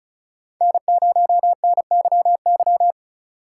Morse code, heard as a keyed tone, about 34 wpm, 700 Hz. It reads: N0NYY
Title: USA Call Signs - 5 Characters - 3x Character Spacing - 35wpm